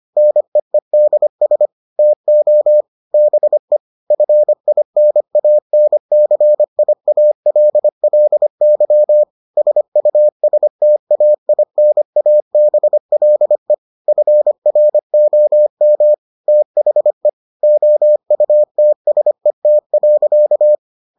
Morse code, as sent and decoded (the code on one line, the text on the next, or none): -. . -.. ... / - --- / -... / ..-. .. -. .- -. -.-. .. .- .-.. .-.. -.-- / ... ..- ... - .- .. -. .- -... .-.. / ..-. .-. --- -- / - .... / --- ..- - ... - .-.-.-
NEEDS TO BE FINANCIALLY SUSTAINABLE FROM THE OUTSET.